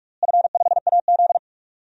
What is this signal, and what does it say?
Morse code, heard as a keyed tone, about 42 wpm, 700 Hz. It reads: F5AZ
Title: DXCC-100 Call Signs - 4 Characters - Repeated 2x - 45wpm